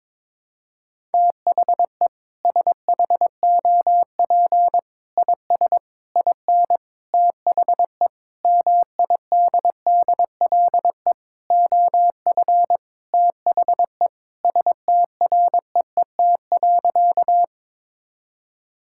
Morse code, heard as a keyed tone, about 22 wpm, 700 Hz. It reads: THE SHOP IS IN THE MIDDLE OF THE STREET.
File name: practice / 22wpm / Sentences with 4-Word Phrases - 22wpm